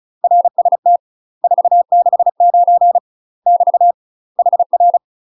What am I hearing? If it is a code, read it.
Morse code, heard as a keyed tone, about 36 wpm, 700 Hz. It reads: RST 469 <BT> HR
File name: UK MASTER QSOs TRAINING / 35WPM / UK MASTER QSOs 0010